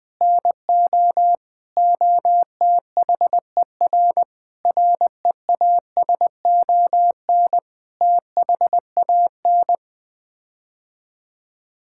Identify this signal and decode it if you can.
Morse code, heard as a keyed tone, about 21 wpm, 700 Hz. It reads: NO OTHER REASON THAN
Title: Sets of 6 Words - 20wpm